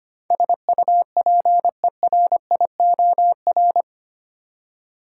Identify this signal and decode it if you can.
Morse code, heard as a keyed tone, about 25 wpm, 690 Hz. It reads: SUPERIOR
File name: 8 Letter Words - First 3 Letters Given - 25wpm